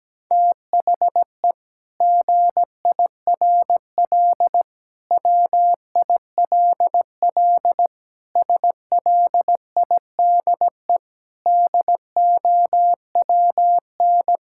Morse code, heard as a keyed tone, about 17 wpm, 700 Hz. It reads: THE GIRL WILL SLIDE DOWN
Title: Sentences from Top 1600 Words 17wpm